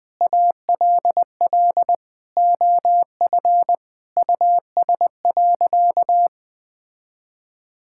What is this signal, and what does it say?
Morse code, heard as a keyed tone, about 20 wpm, 700 Hz. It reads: ALL OF US.
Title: Sentences with 4-Word Phrases - 20wpm